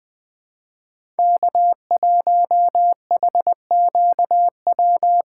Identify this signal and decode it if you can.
Morse code, heard as a keyed tone, about 20 wpm, 700 Hz. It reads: K1HQW
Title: USA Call Signs - 5 Characters - 2x Character Spacing - 20wpm